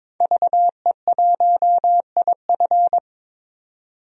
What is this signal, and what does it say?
Morse code, heard as a keyed tone, about 21 wpm, 700 Hz. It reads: VE1IF
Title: DXCC-100 Call Signs - 5 Characters - 22wpm